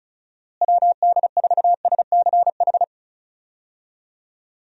Morse code, heard as a keyed tone, about 36 wpm, 700 Hz. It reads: WD4SCH
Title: USA Call Signs - 6 Characters - Repeated 2x - 35wpm